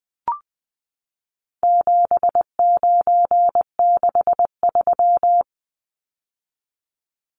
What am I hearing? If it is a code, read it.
Morse code, heard as a keyed tone, about 20 wpm, 700 Hz. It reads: 7963
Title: Numbers - 4-Digit 20wpm